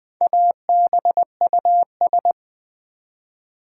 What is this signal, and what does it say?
Morse code, heard as a keyed tone, about 20 wpm, 700 Hz. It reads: ABUS
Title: Build A Word - 1 Letter At A Time - 5-Letter Words - 20wpm